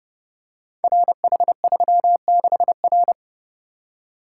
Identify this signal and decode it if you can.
Morse code, heard as a keyed tone, about 29 wpm, 700 Hz. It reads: RH36R
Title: Code Groups - 5 Letter:Numbers - 30wpm